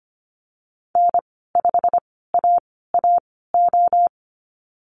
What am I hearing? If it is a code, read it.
Morse code, heard as a keyed tone, about 25 wpm, spaced as for 16 wpm, 690 Hz. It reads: N5AAO
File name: Lesson 15g - Number 5 - Call Signs 25wpm